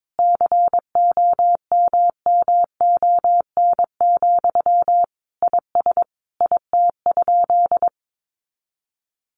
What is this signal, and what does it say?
Morse code, heard as a keyed tone, about 22 wpm, 700 Hz. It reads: COMMON, IS IT?
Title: Sentences from Top 700 Words 22wpm